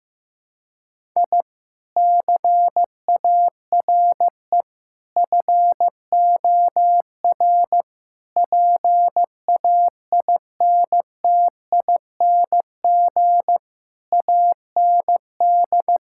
Morse code, heard as a keyed tone, about 15 wpm, 700 Hz. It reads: I CARE FOR PAINTING AND
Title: Sentences from Top 1000 Words - Review Part 6 - 15wpm